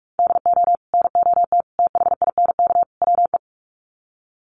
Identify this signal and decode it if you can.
Morse code, heard as a keyed tone, about 44 wpm, 700 Hz. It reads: DO NOT THINK WE
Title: Sets of 4 Words 45wpm